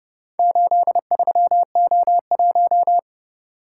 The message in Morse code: 83O1

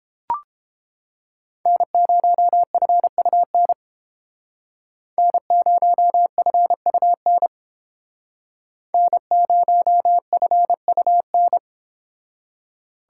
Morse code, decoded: N0FUN N0FUN N0FUN

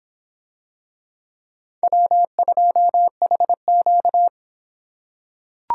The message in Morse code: W2HQ

